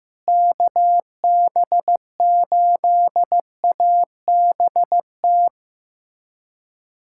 Morse code, heard as a keyed tone, about 15 wpm, 700 Hz. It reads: KB8ABT
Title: USA Callsign DE Callsign 15wpm